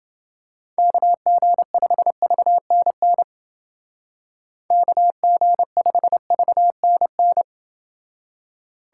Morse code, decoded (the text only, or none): KG5VNN KG5VNN